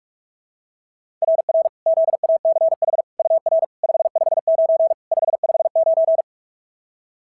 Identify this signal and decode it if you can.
Morse code, heard as a keyed tone, about 41 wpm, 690 Hz. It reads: RR ZACH UR 559 559
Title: SOTA QSO Practice 45wpm